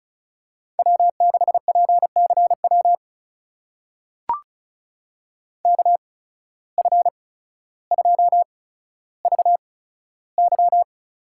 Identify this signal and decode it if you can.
Morse code, heard as a keyed tone, about 36 wpm, spaced as for 15 wpm, 700 Hz. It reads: W6PCW KF2VY
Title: USA Call Signs - 5 Characters - 8x Character Spacing - 35wpm